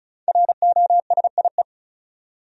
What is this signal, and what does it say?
Morse code, heard as a keyed tone, about 34 wpm, 700 Hz. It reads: ROSIE
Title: USA Common Names - 35wpm